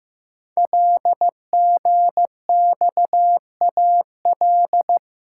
Morse code, heard as a keyed tone, about 15 wpm, 700 Hz. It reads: LGXAL
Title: Code Groups - 5 Letters - 15wpm